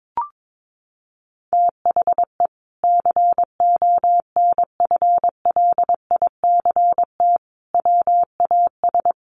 Morse code, heard as a keyed tone, about 22 wpm, 700 Hz. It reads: THE CONFLICT WAS
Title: Sentences from Top 1300 Words 22wpm